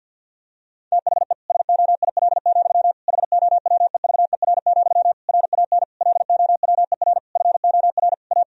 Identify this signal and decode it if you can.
Morse code, heard as a keyed tone, about 49 wpm, 700 Hz. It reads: THE SOIL, HOWEVER, RAN LOWER FOR A
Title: Sentences from Top 1000 Words - Review Part 4 - 50wpm